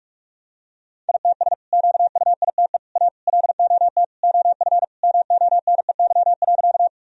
Morse code, heard as a keyed tone, about 43 wpm, 700 Hz. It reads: ITS QUITE A LOT OF MONEY.
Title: Sentences from Top 1000 Words - Review Part 3 - 45wpm